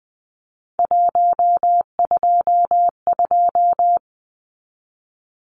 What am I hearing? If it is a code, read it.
Morse code, heard as a keyed tone, about 20 wpm, 700 Hz. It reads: 122
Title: Numbers - 3-Digit 20wpm